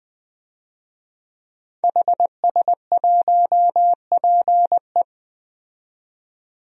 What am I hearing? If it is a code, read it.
Morse code, heard as a keyed tone, about 20 wpm, 700 Hz. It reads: HS1PE
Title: DXCC-100 Call Signs - 5 Characters - 20wpm